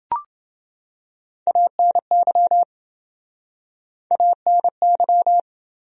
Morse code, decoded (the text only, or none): ANY ANY